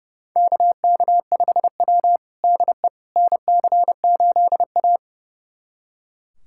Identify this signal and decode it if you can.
Morse code, heard as a keyed tone, about 30 wpm, 700 Hz. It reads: KK5W DE NC8A
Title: USA Callsign DE Callsign - 4 Characters - 30wpm